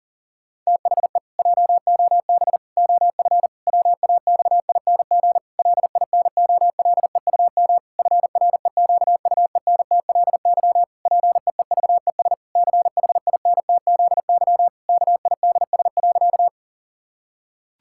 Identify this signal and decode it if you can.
Morse code, heard as a keyed tone, about 38 wpm, 700 Hz. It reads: THE JOB OF WAXING LINOLEUM FREQUENTLY PEEVES CHINTZY KIDS.
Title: Pangram Challenge v2 - 40wpm